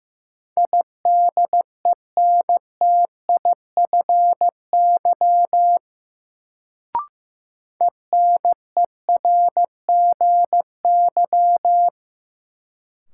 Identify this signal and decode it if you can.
Morse code, heard as a keyed tone, about 15 wpm, 700 Hz. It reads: IDENTIFY ENERGY
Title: Sentences from Top 900 Words - New Word Drill - 15wpm